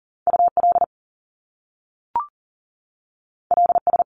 Morse code, heard as a keyed tone, about 40 wpm, 700 Hz. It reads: UL LS
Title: Top Two Letter Combinations in Words 40wpm